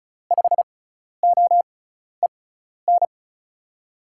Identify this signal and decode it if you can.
Morse code, heard as a keyed tone, about 36 wpm, spaced as for 12 wpm, 700 Hz. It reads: 5OEN
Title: USA Call Signs - 6x Character Spacing - 35wpm